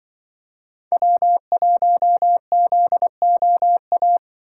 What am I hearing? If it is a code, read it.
Morse code, heard as a keyed tone, about 24 wpm, 700 Hz. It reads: W1ZOA
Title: Speed Racing - USA 1x3 Call Signs 20wpm